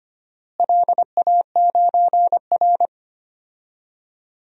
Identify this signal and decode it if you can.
Morse code, heard as a keyed tone, about 25 wpm, 700 Hz. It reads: LA9R
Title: DXCC-100 Call Signs - 4 Characters - Repeated 2x - 25wpm